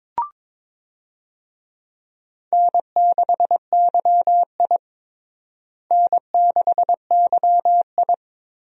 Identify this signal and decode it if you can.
Morse code, heard as a keyed tone, about 22 wpm, 700 Hz. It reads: N6YI N6YI
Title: USA Call Signs - 4 Characters - Repeated 2x - 22wpm